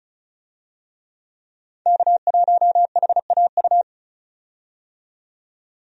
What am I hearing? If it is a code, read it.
Morse code, heard as a keyed tone, about 36 wpm, 700 Hz. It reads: K1HAU
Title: USA Call Signs - 5 Characters - 35wpm